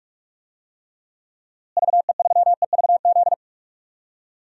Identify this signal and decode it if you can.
Morse code, heard as a keyed tone, about 45 wpm, 700 Hz. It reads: VE3EVZ
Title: DXCC-100 Call Signs - 6 Characters - Repeated 3x - 45wpm